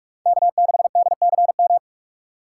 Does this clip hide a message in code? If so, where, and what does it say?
Yes: Morse code from 0.3 to 1.8 s, heard as a keyed tone, about 42 wpm, 700 Hz. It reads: K6DCM